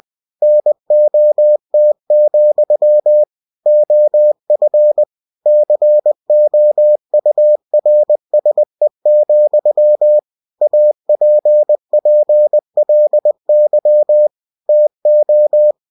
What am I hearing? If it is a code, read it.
Morse code, heard as a keyed tone, about 20 wpm, 600 Hz. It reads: NOT, OF COURSE, APPLY TO
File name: cw-books / winnie-the-pooh / winnie-20wpm-page-0107